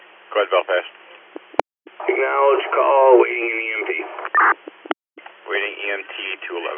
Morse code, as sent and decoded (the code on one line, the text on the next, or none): none